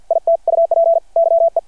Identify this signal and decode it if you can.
Morse code, heard as a keyed tone, about 53 wpm, 670 Hz. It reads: IT UP YE
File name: archived / CRSnet